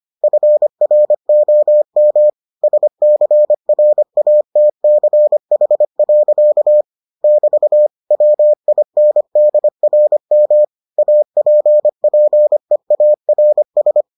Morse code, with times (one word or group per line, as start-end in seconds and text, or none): 0.2-2.3 FROM
2.6-6.8 SCRATCH.
7.2-7.9 <BT>
8.1-10.6 WINDRM
11.0-14.0 APPEARS